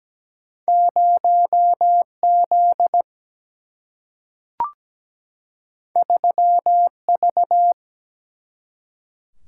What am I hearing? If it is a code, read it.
Morse code, heard as a keyed tone, about 17 wpm, 700 Hz. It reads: 0Z 3V
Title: Single-Letter Single-Number 17wpm